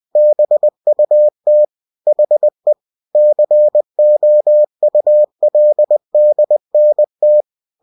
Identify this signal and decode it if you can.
Morse code, heard as a keyed tone, about 20 wpm, 600 Hz. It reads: BUT HE COULDNT
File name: cw-books / winnie-the-pooh / winnie-20wpm-page-0039